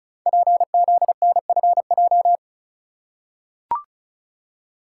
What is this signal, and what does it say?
Morse code, heard as a keyed tone, about 35 wpm, 690 Hz. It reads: PZNFJ
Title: Code Groups - 5 Letters - 35wpm